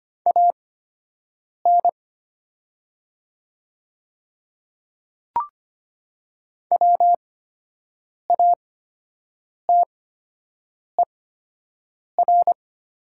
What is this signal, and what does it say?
Morse code, heard as a keyed tone, about 25 wpm, spaced as for 7 wpm, 700 Hz. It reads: AN WATER
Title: 8x Character Spacing - Top 100 Words - 25wpm